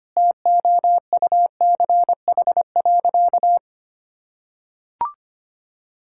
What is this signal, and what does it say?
Morse code, heard as a keyed tone, about 25 wpm, 700 Hz. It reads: TOUCH.
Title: Sentences from Top 1600 Words 25wpm